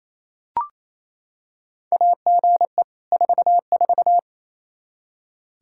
Morse code, heard as a keyed tone, about 27 wpm, 700 Hz. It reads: AGE 44
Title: Practice QSO - Exchange 3 - More Info - 28wpm